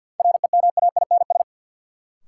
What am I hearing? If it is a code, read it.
Morse code, heard as a keyed tone, about 48 wpm, 700 Hz. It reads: REMAINS
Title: Sets of 5 Words Encore - New Word Drill - 50wpm